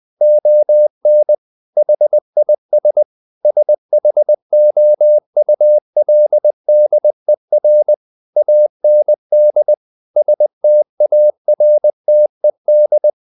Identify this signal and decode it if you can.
Morse code, heard as a keyed tone, about 20 wpm, 600 Hz. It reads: ON HIS SHOULDER AND STARTED
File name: the old man and the sea part 2 20wpm